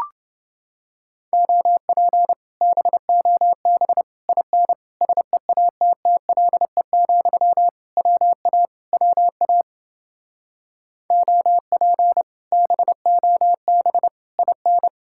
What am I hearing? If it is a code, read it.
Morse code, heard as a keyed tone, about 29 wpm, 700 Hz. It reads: OP BOB IN SEATTLE, WA WA OP BOB IN